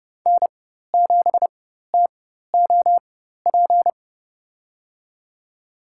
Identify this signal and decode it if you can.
Morse code, heard as a keyed tone, about 30 wpm, spaced as for 14 wpm, 700 Hz. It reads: N7TOP
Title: USA Call Signs - 4x Character Spacing - 30wpm